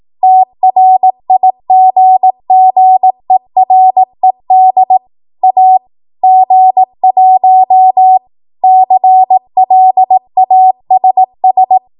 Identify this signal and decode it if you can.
Morse code, heard as a keyed tone, about 18 wpm, 750 Hz. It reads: TRIGGERED A G1 CLASS
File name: Morse / 18WPM